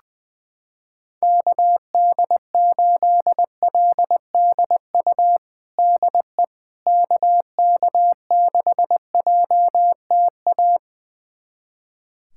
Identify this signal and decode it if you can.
Morse code, heard as a keyed tone, about 20 wpm, 700 Hz. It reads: KD8LDU DE KK6JTA
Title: USA Callsign DE Callsign - 6 Characters - 20wpm